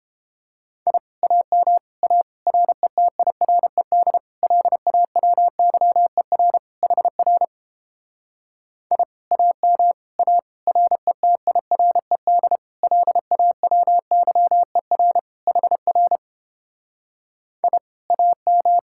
Morse code, heard as a keyed tone, about 31 wpm, 700 Hz. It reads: I AM A RETIRED LAWYER HR I AM A RETIRED LAWYER HR I AM